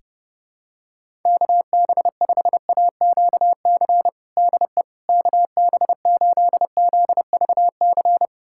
Morse code, heard as a keyed tone, about 30 wpm, 700 Hz. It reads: KB5AQC DE KB8ZVC
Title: USA Callsign DE Callsign - 6 Characters - 30wpm